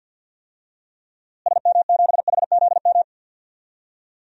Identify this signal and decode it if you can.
Morse code, heard as a keyed tone, about 49 wpm, 700 Hz. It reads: SM7HZM
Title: DXCC-100 Call Signs - 6 Characters - Repeated 3x - 50wpm